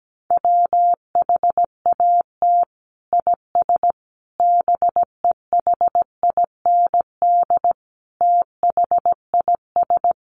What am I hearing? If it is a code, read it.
Morse code, heard as a keyed tone, about 17 wpm, 700 Hz. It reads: WHAT IS BEHIND THIS
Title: Sentences from Top 1100 Words 17wpm